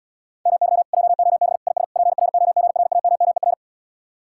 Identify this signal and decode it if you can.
Morse code, heard as a keyed tone, about 79 wpm, 700 Hz. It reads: MY JOB IS PROGRAMMER